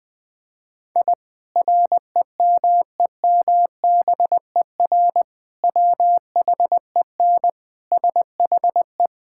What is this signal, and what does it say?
Morse code, heard as a keyed tone, about 20 wpm, 700 Hz. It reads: I REMEMBER WHEN SHE
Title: Sentences with 2-Word Phrases - 20wpm